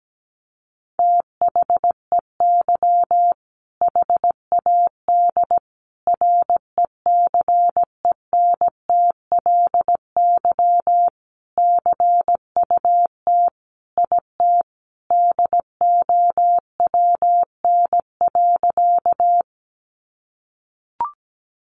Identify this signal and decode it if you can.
Morse code, heard as a keyed tone, about 17 wpm, 700 Hz. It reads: THEY HAD RECENTLY CUT IT DOWN.
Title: Sentences from Top 1200 Words 17wpm